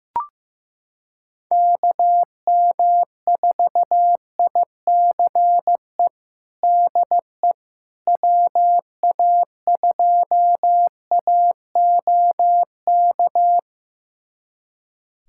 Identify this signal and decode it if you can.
Morse code, heard as a keyed tone, about 15 wpm, 700 Hz. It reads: KM4ICE DE WA2AOK
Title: USA Callsign DE Callsign - 6 Characters - 15wpm